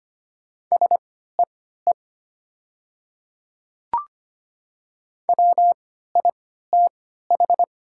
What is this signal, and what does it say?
Morse code, heard as a keyed tone, about 25 wpm, spaced as for 14 wpm, 700 Hz. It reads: SEE WITH
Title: 3x Character Spacing - Top 100 Words - 25wpm